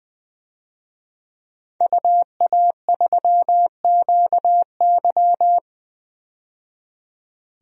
Morse code, heard as a keyed tone, about 20 wpm, 700 Hz. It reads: UA3QY
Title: DXCC-100 Call Signs - 5 Characters - 20wpm